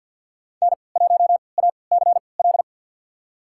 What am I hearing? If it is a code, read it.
Morse code, heard as a keyed tone, about 49 wpm, spaced as for 28 wpm, 700 Hz. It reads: N1ACL
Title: USA Call Signs - 5 Characters - 3x Character Spacing - 50wpm